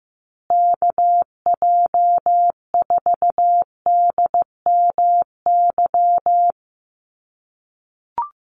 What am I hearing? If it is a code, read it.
Morse code, heard as a keyed tone, about 15 wpm, 700 Hz. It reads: KJ4DMY
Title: USA Call Signs Repeated 2x 15wpm